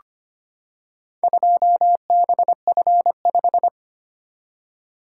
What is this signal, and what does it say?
Morse code, heard as a keyed tone, about 25 wpm, 690 Hz. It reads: 2BF5
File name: practice / 25wpm / Four Letter-Number Combinations - 25wpm